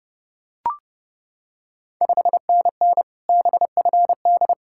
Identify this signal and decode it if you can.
Morse code, heard as a keyed tone, about 32 wpm, 700 Hz. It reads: 5NN BFD
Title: Ontario QSO Party Exchange 30wpm